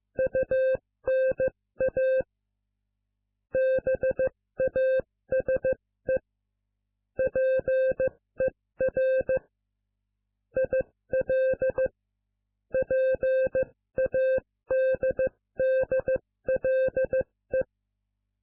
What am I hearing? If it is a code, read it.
Morse code, heard as a keyed tone, about 15 wpm, spaced as for 13 wpm, 520 Hz. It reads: UNA BASE PER IL PADDLE